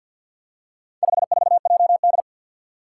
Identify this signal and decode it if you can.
Morse code, heard as a keyed tone, about 52 wpm, 700 Hz. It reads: 54JD
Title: Four Challenging Letter-Number Combinations 50wpm